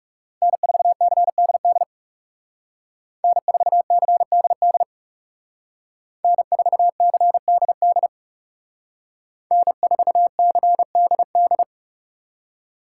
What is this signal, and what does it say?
Morse code, heard as a keyed tone, about 38 wpm, 700 Hz. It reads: N4CDD N4CDD N4CDD N4CDD